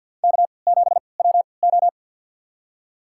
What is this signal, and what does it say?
Morse code, heard as a keyed tone, about 52 wpm, spaced as for 28 wpm, 700 Hz. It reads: K7WO
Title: USA Call Signs - 4 Characters - 3x Character Spacing - 50wpm